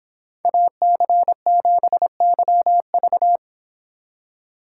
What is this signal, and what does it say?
Morse code, heard as a keyed tone, about 25 wpm, 700 Hz. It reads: AC7YV